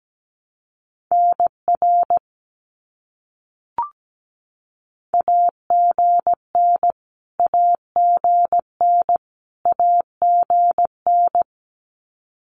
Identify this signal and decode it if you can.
Morse code, heard as a keyed tone, about 17 wpm, 700 Hz. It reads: NR AGN AGN AGN
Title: Abbreviations Repeated 3x - 17wpm